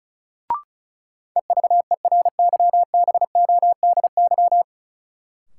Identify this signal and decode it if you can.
Morse code, heard as a keyed tone, about 36 wpm, 700 Hz. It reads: EVERYBODY
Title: Sentences from Top 1400 Words - New Word Drill - 35wpm